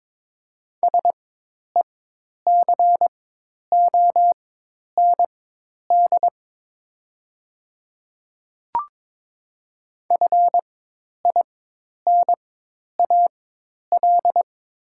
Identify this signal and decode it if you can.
Morse code, heard as a keyed tone, about 22 wpm, spaced as for 10 wpm, 700 Hz. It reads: SECOND FINAL
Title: 4x Character Spacing - Top 500 Words - 22wpm